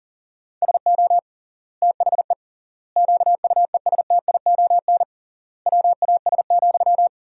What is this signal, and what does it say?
Morse code, heard as a keyed tone, about 40 wpm, 700 Hz. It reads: SO THE QUESTION WAS,
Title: 3x Word-Spacing - Sentences from Top 400 Words - 40wpm